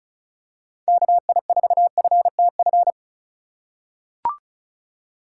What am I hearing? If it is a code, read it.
Morse code, heard as a keyed tone, about 35 wpm, 700 Hz. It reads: KI4FTF
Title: USA Call Signs - 6 Characters - 3x Character Spacing - 35wpm